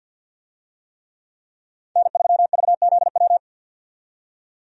Morse code, heard as a keyed tone, about 52 wpm, 700 Hz. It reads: N3VZW
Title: USA Call Signs - 5 Characters - 50wpm